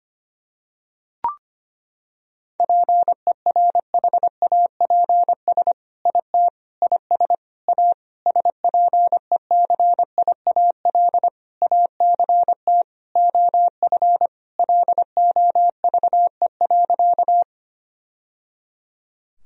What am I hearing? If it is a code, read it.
Morse code, heard as a keyed tone, about 25 wpm, 700 Hz. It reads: PERHAPS IT IS A SPECIAL ACT OF LOVE.
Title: Sentences from Top 1000 Words - Review Part 3 - 25wpm